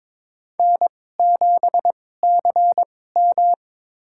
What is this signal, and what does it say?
Morse code, heard as a keyed tone, about 22 wpm, spaced as for 16 wpm, 700 Hz. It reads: N7CM
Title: USA Call Signs - 4 Characters - 2x Character Spacing - 22wpm